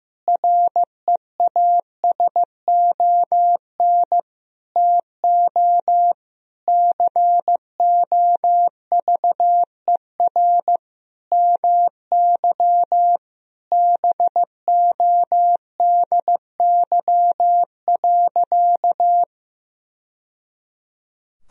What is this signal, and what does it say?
Morse code, heard as a keyed tone, about 15 wpm, 700 Hz. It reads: REASON TO COVER MY BODY.
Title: Sentences from Top 1000 Words - Review Part 1 - 15wpm